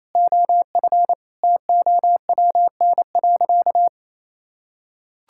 Morse code, heard as a keyed tone, about 28 wpm, 700 Hz. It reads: OF TOWN.
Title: Sentences from Top 1600 Words 28wpm